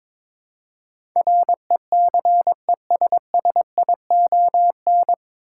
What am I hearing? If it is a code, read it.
Morse code, heard as a keyed tone, about 22 wpm, 700 Hz. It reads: RECESSION
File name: 9 Letter Words - First 3 Letters Given - 22wpm